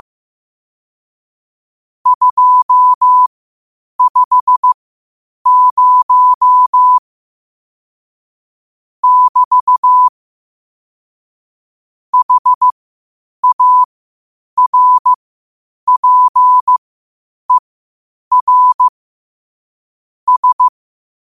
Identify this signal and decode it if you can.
Morse code, heard as a keyed tone, about 15 wpm, spaced as for 8 wpm, 1000 Hz. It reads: 250 <BT> HARPER S
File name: qotdfiles / 8wpm